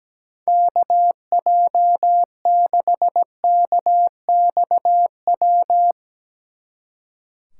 Morse code, heard as a keyed tone, about 18 wpm, 700 Hz. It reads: KJ6KXW
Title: Speed Racing - USA 2x3 Call Signs 17wpm